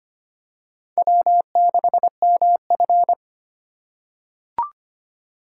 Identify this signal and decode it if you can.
Morse code, heard as a keyed tone, about 25 wpm, 700 Hz. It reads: W6MF
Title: USA Call Signs - 4 Characters - 3x Character Spacing - 25wpm